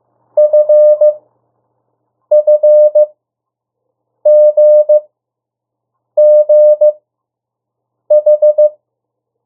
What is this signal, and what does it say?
Morse code, heard as a keyed tone, about 15 wpm, spaced as for 6 wpm, 600 Hz. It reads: FFGGH
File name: morse mp3 / 02 urok 1